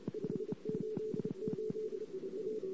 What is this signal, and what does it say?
Morse code, heard as a keyed tone, about 28 wpm, 440 Hz, with noise. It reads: 588 A
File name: CW / 2008-ss-cw-k5tr-op-n5rz -14:00